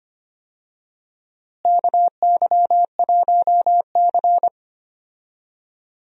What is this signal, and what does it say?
Morse code, heard as a keyed tone, about 25 wpm, 700 Hz. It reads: KY1C
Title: USA Call Signs - 4 Characters - Repeated 2x - 25wpm